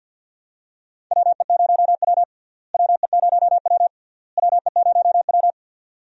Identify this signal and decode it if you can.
Morse code, heard as a keyed tone, about 52 wpm, 700 Hz. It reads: WE0W WE0W WE0W
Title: USA Call Signs - 4 Characters - Repeated 3x - 50wpm